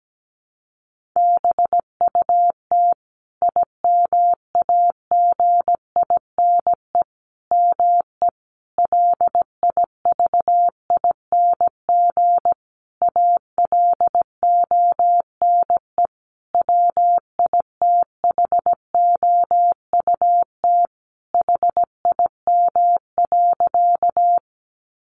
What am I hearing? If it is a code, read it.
Morse code, heard as a keyed tone, about 17 wpm, 700 Hz. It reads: BUT IMAGINE ME LIVING ALONE WITHOUT HIM.
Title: Sentences from Top 1400 Words 17wpm